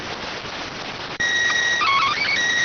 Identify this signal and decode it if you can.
SSTV (slow-scan television), PD 290.